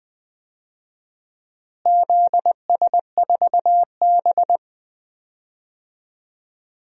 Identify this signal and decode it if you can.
Morse code, heard as a keyed tone, about 20 wpm, 700 Hz. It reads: ZS4B
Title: DXCC-100 Call Signs - 4 Characters - Repeated 2x - 20wpm